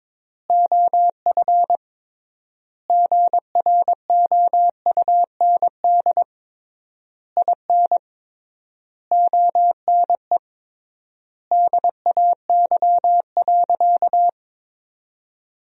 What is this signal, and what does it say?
Morse code, heard as a keyed tone, about 22 wpm, 700 Hz. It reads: OF GROUND IN ONE DAY.
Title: 3x Word-Spacing - Sentences from Top 500 Words - 22wpm